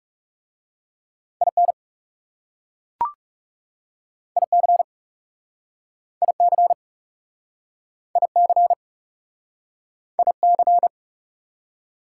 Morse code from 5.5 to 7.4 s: IC

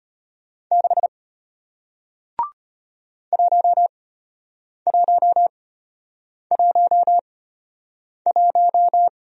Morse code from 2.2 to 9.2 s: 1111

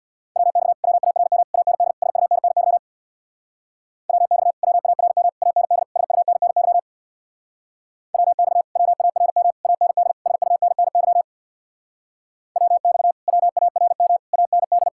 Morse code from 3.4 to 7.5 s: WX WARM AND SUNNY

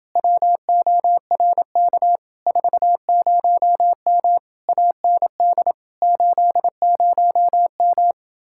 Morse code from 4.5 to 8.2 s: AND 80M

